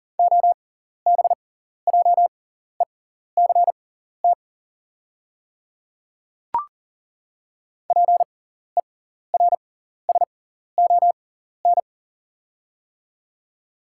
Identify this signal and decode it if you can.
Morse code, heard as a keyed tone, about 40 wpm, spaced as for 14 wpm, 690 Hz. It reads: OBJECT PERSON
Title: 6x Character Spacing - Top 500 Words - 40wpm